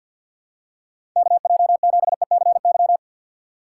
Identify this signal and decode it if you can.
Morse code, heard as a keyed tone, about 50 wpm, 700 Hz. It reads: KJ7ECY
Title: USA Call Signs - 6 Characters - 4x Character Spacing - 50wpm